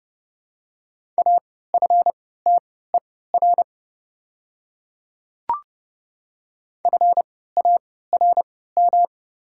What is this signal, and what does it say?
Morse code, heard as a keyed tone, about 30 wpm, spaced as for 17 wpm, 700 Hz. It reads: AFTER FARM
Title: 3x Character Spacing - Top 500 Words - 30wpm